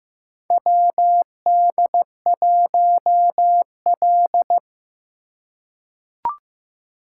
Morse code, heard as a keyed tone, about 15 wpm, 700 Hz. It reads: WD1L
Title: USA Call Signs - 4 Characters - 2x Character Spacing - 15wpm